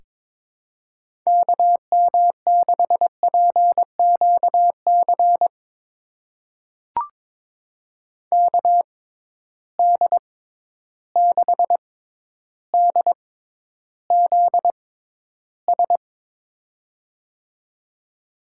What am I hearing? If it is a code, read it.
Morse code, heard as a keyed tone, about 21 wpm, spaced as for 11 wpm, 700 Hz. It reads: KM6PQC KD6DZS
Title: USA Call Signs - 6 Characters - 6x Character Spacing - 22wpm